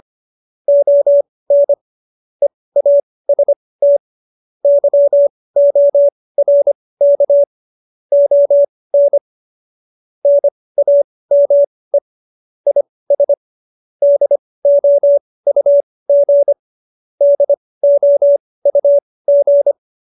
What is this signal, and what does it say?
Morse code, heard as a keyed tone, about 25 wpm, spaced as for 18 wpm, 570 Hz. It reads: ON EAST YORK ON NAME IS DOUG DOUG